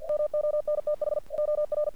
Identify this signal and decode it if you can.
Morse code, heard as a keyed tone, about 49 wpm, 600 Hz. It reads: MONTH OR
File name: archived / CRSnet